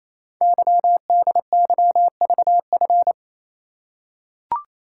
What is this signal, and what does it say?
Morse code, heard as a keyed tone, about 27 wpm, 700 Hz. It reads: YDYVF
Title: Code Groups - 5 Letters - 28wpm